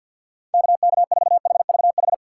Morse code, heard as a keyed tone, about 46 wpm, 700 Hz. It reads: KK4HVH
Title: USA Call Signs - 6x Character Spacing - 50wpm